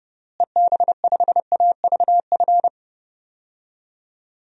Morse code, heard as a keyed tone, about 31 wpm, 700 Hz. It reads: EB5AVF
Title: DXCC-100 Call Signs - 6 Characters - Repeated 2x - 30wpm